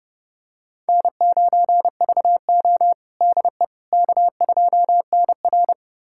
Morse code, heard as a keyed tone, about 31 wpm, 700 Hz. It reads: N9VO DE K2NR
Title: USA Callsign DE Callsign - 4 Characters - 30wpm